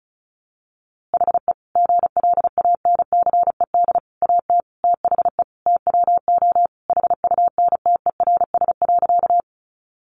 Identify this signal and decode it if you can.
Morse code, heard as a keyed tone, about 34 wpm, 700 Hz. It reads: HE GLANCED AT THE TWO HUNTERS.
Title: A Sound Of Thunder 35wpm